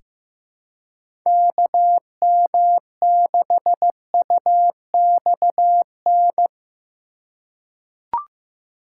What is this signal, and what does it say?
Morse code, heard as a keyed tone, about 15 wpm, 700 Hz. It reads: KM6UXN
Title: USA Call Signs - 6 Characters - 3x Character Spacing - 15wpm